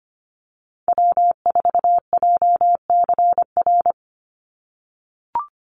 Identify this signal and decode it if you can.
Morse code, heard as a keyed tone, about 25 wpm, 700 Hz. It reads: W4JCR
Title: USA Call Signs 25wpm